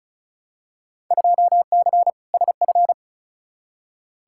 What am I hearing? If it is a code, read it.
Morse code, heard as a keyed tone, about 36 wpm, 690 Hz. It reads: 2C SF
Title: Field Day Exchanges Encore - 35wpm